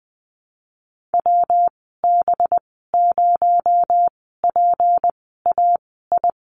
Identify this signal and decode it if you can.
Morse code, heard as a keyed tone, about 20 wpm, spaced as for 15 wpm, 700 Hz. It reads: WB0PAI
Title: 231 Lesson 37g - Letter x - Call Signs 20wpm